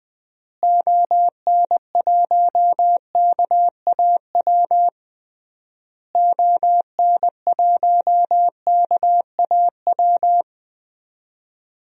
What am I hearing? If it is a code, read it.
Morse code, heard as a keyed tone, about 20 wpm, 700 Hz. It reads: ON1KAW ON1KAW
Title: DXCC-100 Call Signs - 6 Characters - Repeated 3x - 20wpm